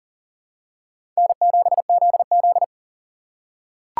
Morse code, heard as a keyed tone, about 40 wpm, 700 Hz. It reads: N7ZZ
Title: USA Call Signs - 2x Character Spacing - 40wpm